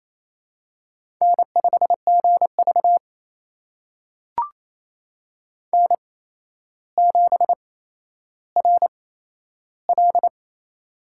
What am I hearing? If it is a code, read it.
Morse code, heard as a keyed tone, about 28 wpm, spaced as for 12 wpm, 700 Hz. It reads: N5GV N7RL
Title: USA Call Signs - 4 Characters - 8x Character Spacing - 28wpm